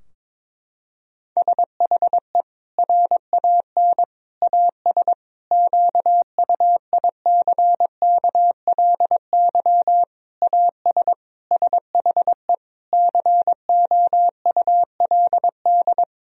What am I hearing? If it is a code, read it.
Morse code, heard as a keyed tone, about 22 wpm, 700 Hz. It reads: SHE RAN AS QUICKLY AS SHE COULD
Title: Sentences from Top 1000 Words - Review Part 4 - 22wpm